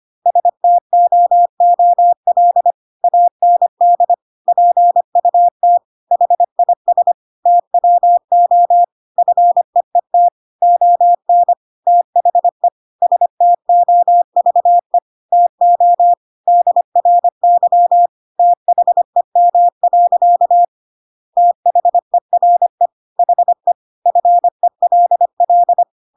Morse code, heard as a keyed tone, about 25 wpm, 700 Hz. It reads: STOOL AND PUT HIS TWO FEET ON THE STOVE TO DRY THEM. THERE HE FELL